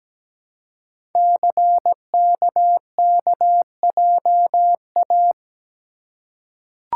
Morse code, heard as a keyed tone, about 17 wpm, 700 Hz. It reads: CKKJA
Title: Code Groups - 5 Letters - 17wpm